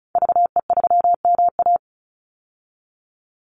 Morse code, heard as a keyed tone, about 34 wpm, 700 Hz. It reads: VE3MA
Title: Call Signs Repeated 3x - 35wpm